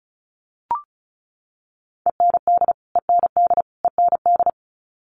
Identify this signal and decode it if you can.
Morse code, heard as a keyed tone, about 34 wpm, 700 Hz. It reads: END END END
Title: Top 200 Words Repeated 35wpm